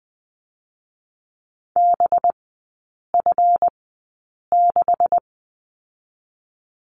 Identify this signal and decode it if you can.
Morse code, heard as a keyed tone, about 20 wpm, spaced as for 8 wpm, 700 Hz. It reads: BF6
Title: 184 Lesson 30e - Letter b - Three Characters 20wpm